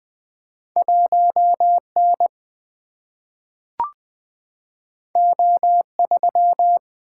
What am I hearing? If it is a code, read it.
Morse code, heard as a keyed tone, about 20 wpm, 700 Hz. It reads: 1N O3
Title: Single-Letter Single-Number 20wpm